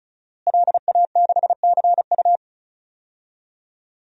Morse code, heard as a keyed tone, about 34 wpm, 700 Hz. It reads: LA6CU
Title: DXCC-100 Call Signs - 5 Characters - Repeated 3x - 35wpm